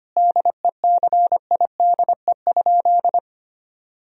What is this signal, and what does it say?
Morse code, heard as a keyed tone, about 25 wpm, 700 Hz. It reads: DECIDE?
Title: Sentences from Top 1000 Words - Review Part 3 - 25wpm